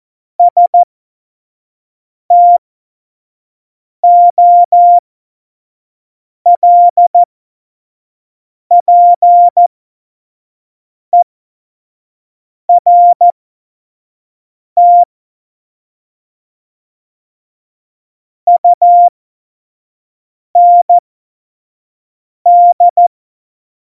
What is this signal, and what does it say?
Morse code, heard as a keyed tone, about 14 wpm, spaced as for 5 wpm, 700 Hz. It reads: STOLPERT UND
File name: Morsekurs / Witze / Witz 24. November 2020, 25 F70